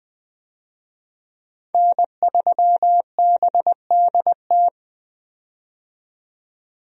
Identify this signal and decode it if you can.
Morse code, heard as a keyed tone, about 20 wpm, 700 Hz. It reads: N3BDT
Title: USA Call Signs - 5 Characters - 20wpm